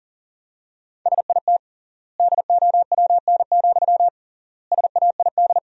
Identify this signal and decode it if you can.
Morse code, heard as a keyed tone, about 37 wpm, 700 Hz. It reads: SIT DOWN, SAID
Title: 3x Word-Spacing - Sentences from Top 500 Words - 40wpm